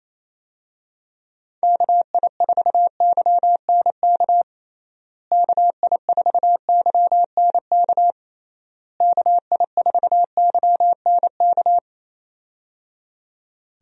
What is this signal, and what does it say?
Morse code, heard as a keyed tone, about 27 wpm, 700 Hz. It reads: KI4YNK KI4YNK KI4YNK
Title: USA Call Signs - 6 Characters - Repeated 3x - 28wpm